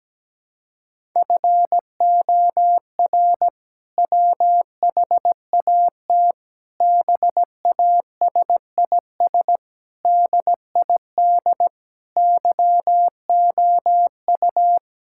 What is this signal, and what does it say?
Morse code, heard as a keyed tone, about 17 wpm, 700 Hz. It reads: FOR WHAT BASIS DID YOU
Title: Sentences from Top 1000 Words - Review Part 5 - 17wpm